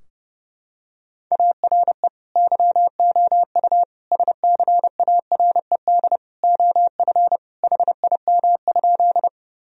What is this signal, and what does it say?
Morse code, heard as a keyed tone, about 30 wpm, 700 Hz. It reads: ARE YOU SCARED OF HIM?
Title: Sentences from Top 1000 Words - Review Part 4 - 30wpm